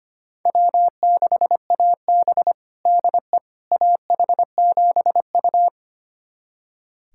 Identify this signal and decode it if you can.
Morse code, heard as a keyed tone, about 25 wpm, 700 Hz. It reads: W6AB DE AH7U
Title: USA Callsign DE Callsign - 4 Characters - 25wpm